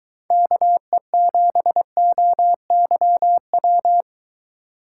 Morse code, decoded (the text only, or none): KE7OYW